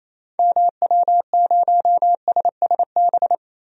Morse code, heard as a keyed tone, about 28 wpm, 700 Hz. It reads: MW0SSB
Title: DXCC-100 Call Signs - 6 Characters - Repeated 3x - 28wpm